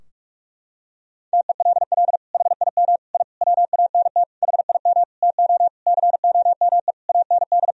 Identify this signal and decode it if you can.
Morse code, heard as a keyed tone, about 46 wpm, 700 Hz. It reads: TELL HIM I WANT HIM TO COME AND
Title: Sentences from Top 1000 Words - Review Part 2 - 45wpm